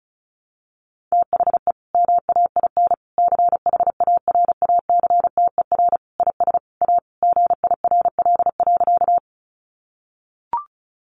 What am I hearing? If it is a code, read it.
Morse code, heard as a keyed tone, about 34 wpm, 700 Hz. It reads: THE MAIN CHARACTER IS A GIRL.
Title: Additional Easy Phrases 35wpm